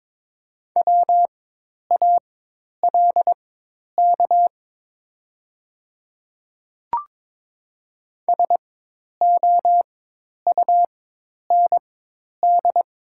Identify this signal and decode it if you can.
Morse code, heard as a keyed tone, about 23 wpm, spaced as for 10 wpm, 700 Hz. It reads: WALK SOUND
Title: 4x Character Spacing - Top 300 Words - 22wpm